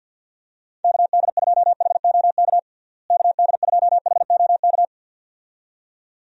Morse code, heard as a keyed tone, about 50 wpm, 700 Hz. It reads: KD2HOK KD2HOK
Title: USA Call Signs - 6 Characters - Repeated 2x - 50wpm